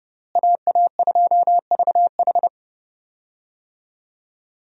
Morse code, heard as a keyed tone, about 30 wpm, 700 Hz. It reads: AA2VH
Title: USA Call Signs - 5 Characters - Repeated 3x - 30wpm